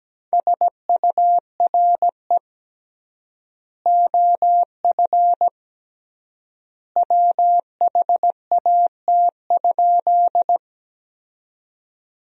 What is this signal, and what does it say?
Morse code, heard as a keyed tone, about 17 wpm, 700 Hz. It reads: SURE OF WHAT?
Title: 3x Word-Spacing - Sentences from Top 300 Words - 17wpm